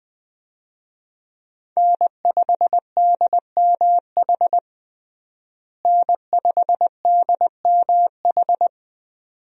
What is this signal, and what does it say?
Morse code, heard as a keyed tone, about 20 wpm, 700 Hz. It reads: N5DMH N5DMH
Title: USA Call Signs - 5 Characters - Repeated 2x - 20wpm